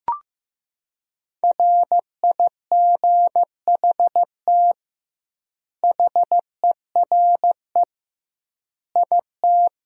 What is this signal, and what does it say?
Morse code, heard as a keyed tone, about 15 wpm, 700 Hz. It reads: RIGHT HERE IT
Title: 2x Word-Spacing - Sentences from Top 300 Words - 15wpm